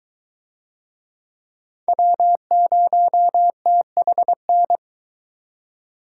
Morse code, heard as a keyed tone, about 22 wpm, 700 Hz. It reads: W0THN